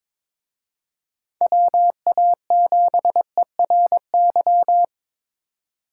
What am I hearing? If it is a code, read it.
Morse code, heard as a keyed tone, about 26 wpm, 700 Hz. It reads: WA7ERY
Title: Speed Racing - USA 2x3 Call Signs 22wpm